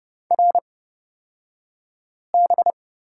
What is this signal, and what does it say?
Morse code, heard as a keyed tone, about 30 wpm, spaced as for 5 wpm, 700 Hz. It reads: RB
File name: course / 30:10wpm / 235 Lesson 38d - Letter q - Two Characters 30wpm